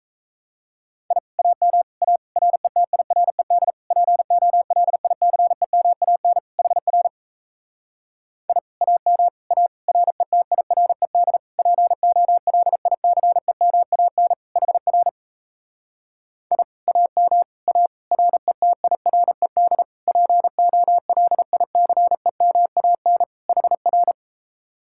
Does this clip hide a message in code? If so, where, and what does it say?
Yes: Morse code from 1.1 to 24.1 s, heard as a keyed tone, about 38 wpm, 700 Hz. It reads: I AM A RETIRED POLICEMAN HR I AM A RETIRED POLICEMAN HR I AM A RETIRED POLICEMAN HR